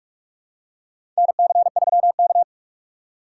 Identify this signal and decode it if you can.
Morse code, heard as a keyed tone, about 44 wpm, 690 Hz. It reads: NC3K
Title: USA Call Signs - 4 Characters - 2x Character Spacing - 45wpm